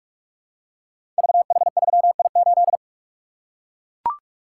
Morse code, heard as a keyed tone, about 44 wpm, 700 Hz. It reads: VH3I8
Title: Code Groups - 5 Letter:Numbers - 45wpm